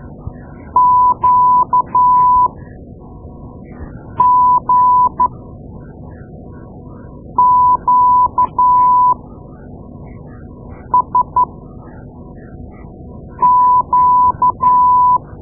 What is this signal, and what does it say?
Morse code, heard as a keyed tone, about 12 wpm, spaced as for 4 wpm, 1000 Hz. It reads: QGQSQ
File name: Morse Lessons / Morse Lessons / Tape-2A